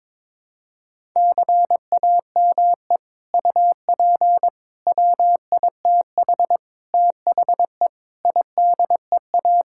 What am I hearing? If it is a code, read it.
Morse code, heard as a keyed tone, about 22 wpm, 700 Hz. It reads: CAME UP WITH THE IDEA
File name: practice / 22wpm / Sets of 6 Words - 22wpm